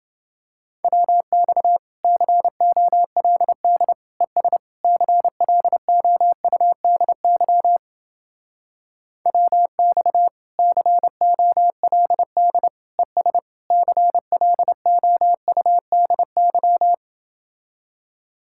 Morse code, heard as a keyed tone, about 30 wpm, 700 Hz. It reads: WX COLD ES CLOUDY WX COLD ES CLOUDY